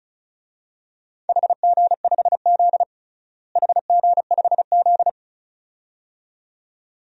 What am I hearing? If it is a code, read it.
Morse code, heard as a keyed tone, about 36 wpm, 700 Hz. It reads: HG5Z HG5Z
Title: DXCC-100 Call Signs - 4 Characters - Repeated 2x - 35wpm